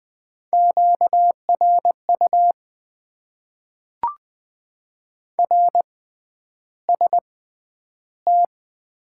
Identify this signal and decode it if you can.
Morse code, heard as a keyed tone, about 20 wpm, spaced as for 10 wpm, 700 Hz. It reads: QRU RST
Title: Top 90 QSO Elements - 6x Character Spacing - 20wpm